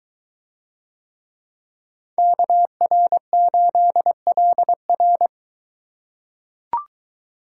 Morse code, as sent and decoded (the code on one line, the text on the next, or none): -.- .-. ---.. .-.. .-.
KR8LR